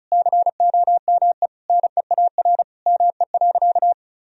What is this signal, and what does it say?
Morse code, heard as a keyed tone, about 34 wpm, 700 Hz. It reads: COME NEAR ME.
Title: Sentences from Top 1000 Words - Review Part 1 - 35wpm